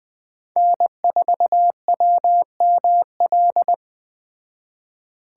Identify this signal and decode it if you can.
Morse code, heard as a keyed tone, about 20 wpm, 700 Hz. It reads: N4WML
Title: Speed Racing - USA 1x3 Call Signs 17wpm